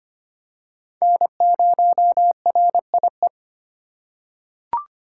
Morse code, heard as a keyed tone, about 25 wpm, 700 Hz. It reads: N0RIE
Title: USA Call Signs - 5 Characters - 4x Character Spacing - 25wpm